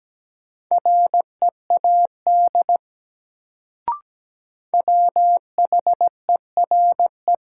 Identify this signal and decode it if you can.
Morse code, heard as a keyed tone, about 17 wpm, 700 Hz. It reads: READ WHERE
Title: Sentences from Top 200 Words - New Word Drill - 17wpm